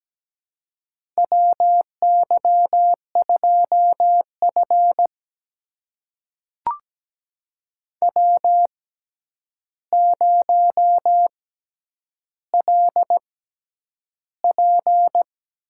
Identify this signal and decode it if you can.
Morse code, heard as a keyed tone, about 17 wpm, spaced as for 9 wpm, 700 Hz. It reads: WY2F W0LP
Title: USA Call Signs - 6x Character Spacing - 17wpm